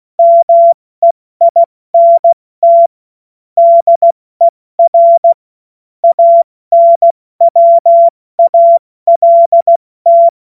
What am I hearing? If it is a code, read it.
Morse code, heard as a keyed tone, about 16 wpm, spaced as for 14 wpm, 680 Hz. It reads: MEINT DER ANWALT
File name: Morsekurs / Witze / Witz 29. August 2023 70 F80